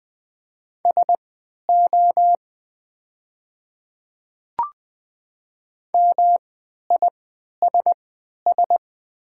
Morse code, heard as a keyed tone, about 20 wpm, spaced as for 11 wpm, 700 Hz. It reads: SO MISS
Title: 3x Character Spacing - Top 500 Words - 20wpm